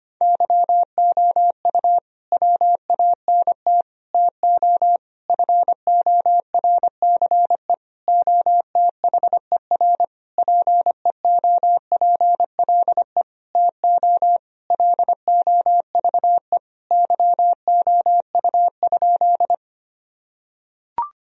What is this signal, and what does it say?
Morse code, heard as a keyed tone, about 25 wpm, 700 Hz. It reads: YOU WANT TO FORCE OTHER PEOPLE TO LOVE YOU?
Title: Sentences from Top 1000 Words - Review Part 3 - 25wpm